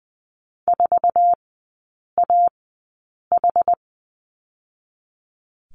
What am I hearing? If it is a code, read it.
Morse code, heard as a keyed tone, about 20 wpm, spaced as for 8 wpm, 700 Hz. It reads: 4AH
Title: 060 Lesson 11e - Letter h - Three Characters 20wpm